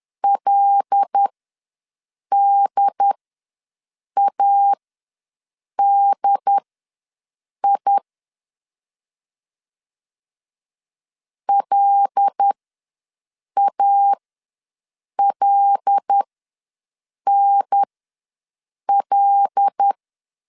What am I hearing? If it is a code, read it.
Morse code, heard as a keyed tone, about 10 wpm, spaced as for 6 wpm, 800 Hz. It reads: LDADI LALNL
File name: LEZIONI / 5 LBVC / 5 L ws 10wpm 3636